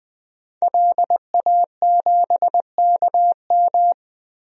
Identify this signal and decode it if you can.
Morse code, heard as a keyed tone, about 20 wpm, 700 Hz. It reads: LA7KM
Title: DXCC-100 Call Signs - 5 Characters - Repeated 2x - 20wpm